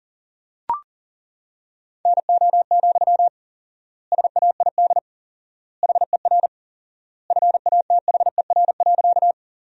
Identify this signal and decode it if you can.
Morse code, heard as a keyed tone, about 40 wpm, 700 Hz. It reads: NO, SAID HER FATHER.
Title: 4x Word-Spacing - Sentences from Top 300 Words - 40wpm